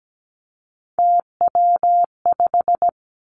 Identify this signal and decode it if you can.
Morse code, heard as a keyed tone, about 17 wpm, 700 Hz. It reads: TW5
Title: Three Challenging Letter-Number Combinations 17wpm